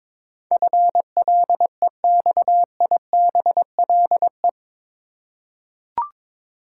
Morse code, heard as a keyed tone, about 22 wpm, 700 Hz. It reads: FLEXIBLE
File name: practice / 22wpm / Eight Letter Words - 22wpm